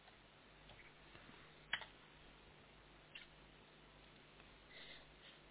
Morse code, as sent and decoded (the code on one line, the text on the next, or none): none